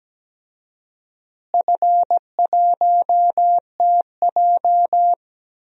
Morse code, heard as a keyed tone, about 17 wpm, 700 Hz. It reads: F1TJ
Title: DXCC-100 Call Signs - 4 Characters - Repeated 3x - 17wpm